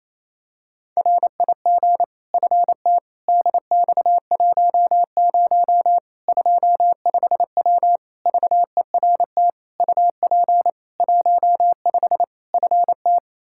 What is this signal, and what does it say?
Morse code, heard as a keyed tone, about 28 wpm, 700 Hz. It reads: RIG FT DX10 25W VERT UP 15 FT
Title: K1USN SST QSO Practice 28wpm